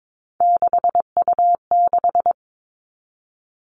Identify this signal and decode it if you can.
Morse code, heard as a keyed tone, about 22 wpm, 700 Hz. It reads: 6U6
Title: All Cut Numbers - 3-Digit 22wpm